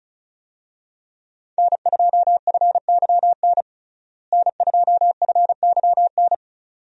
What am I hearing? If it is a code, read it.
Morse code, heard as a keyed tone, about 34 wpm, 700 Hz. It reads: N2FYN N2FYN
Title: USA Call Signs - 5 Characters - Repeated 2x - 35wpm